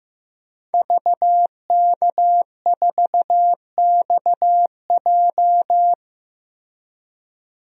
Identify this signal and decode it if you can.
Morse code, heard as a keyed tone, about 15 wpm, 700 Hz. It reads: VK4XJ
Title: DXCC-100 Call Signs - 5 Characters - 15wpm